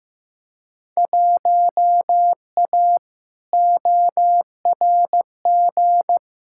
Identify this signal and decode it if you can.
Morse code, heard as a keyed tone, about 15 wpm, 700 Hz. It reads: 1A ORG
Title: Field Day Exchanges Encore - 15wpm